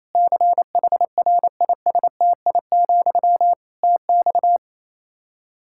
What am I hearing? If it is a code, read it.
Morse code, heard as a keyed tone, about 28 wpm, 700 Hz. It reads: CHRISTI, TX
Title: K1USN SST QSO Practice 28wpm